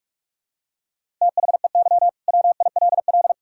Morse code, heard as a keyed tone, about 41 wpm, 700 Hz. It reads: THEY WILL
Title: Sentences from Top 1000 Words - Review Part 5 - 45wpm